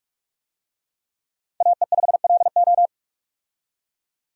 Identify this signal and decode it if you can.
Morse code, heard as a keyed tone, about 46 wpm, 700 Hz. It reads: AE5LO
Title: USA Call Signs - 5 Characters - 45wpm